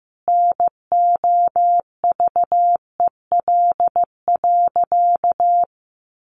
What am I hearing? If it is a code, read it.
Morse code, heard as a keyed tone, about 14 wpm, 700 Hz. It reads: NOVEL.
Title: Sentences from Top 1300 Words 15wpm